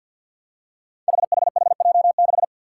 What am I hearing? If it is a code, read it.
Morse code, heard as a keyed tone, about 49 wpm, 700 Hz. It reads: HHHJ6
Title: Five Challenging Letter-Number Combinations 50wpm